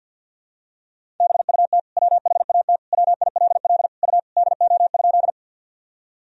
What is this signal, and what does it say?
Morse code, heard as a keyed tone, about 52 wpm, 700 Hz. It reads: BUT WHAT WILL U DO?
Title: Sentences from Top 100 Words and Abbreviations 50wpm